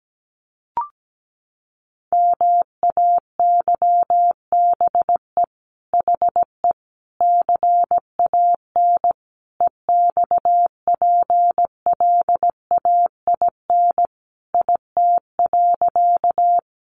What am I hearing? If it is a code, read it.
Morse code, heard as a keyed tone, about 17 wpm, 700 Hz. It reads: MAYBE HE CAN EXPLAIN IT.
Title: Sentences from Top 1100 Words 17wpm